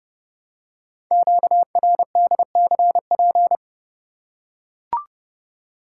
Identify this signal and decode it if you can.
Morse code, heard as a keyed tone, about 30 wpm, 700 Hz. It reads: QRDCP
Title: Code Groups - 5 Letter:Numbers - 30wpm